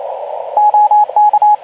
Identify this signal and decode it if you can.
Morse code, heard as a keyed tone, about 28 wpm, 840 Hz, with radio noise. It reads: OK
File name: Archiv / CUC189 / ok2zcq cuc 189